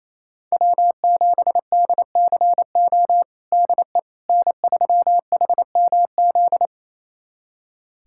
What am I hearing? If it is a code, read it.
Morse code, heard as a keyed tone, about 28 wpm, 700 Hz. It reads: W7DCO DE N3HMZ
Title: USA Callsign DE Callsign - 5 Characters - 28wpm